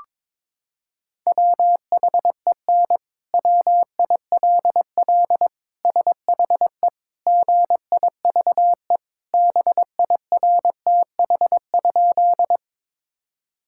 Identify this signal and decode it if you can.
Morse code, heard as a keyed tone, about 22 wpm, 700 Hz. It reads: WHEN WILL SHE GIVE BIRTH?
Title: Sentences from Top 1000 Words - Review Part 5 - 22wpm